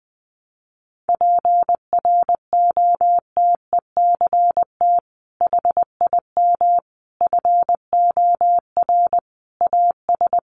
Morse code, heard as a keyed tone, about 20 wpm, 700 Hz. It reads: PROTECT HIM FOR AS
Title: Sentences from Top 900 Words 20wpm